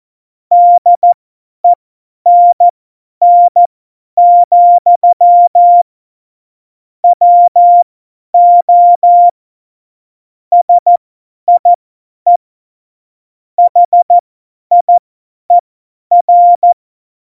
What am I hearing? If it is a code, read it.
Morse code, heard as a keyed tone, about 14 wpm, spaced as for 10 wpm, 700 Hz. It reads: DENN, WO SIE HIER
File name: Morsekurs / Witze / Witz 15. September 2020, 50 F70